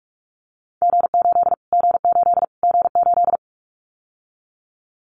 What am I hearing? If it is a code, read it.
Morse code, heard as a keyed tone, about 44 wpm, 700 Hz. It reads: G8 G8 G8
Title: Callsign Prefixes for Top 25 DXCC - Repeated 3x - 45wpm